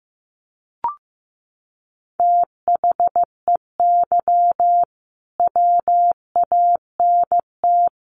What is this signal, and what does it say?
Morse code, heard as a keyed tone, about 15 wpm, 700 Hz. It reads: THEY WANT
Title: Sentences from Top 200 Words 15wpm